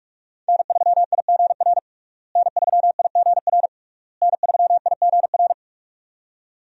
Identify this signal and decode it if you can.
Morse code, heard as a keyed tone, about 46 wpm, 700 Hz. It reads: N3IGR N3IGR N3IGR
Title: USA Call Signs - 5 Characters - Repeated 3x - 45wpm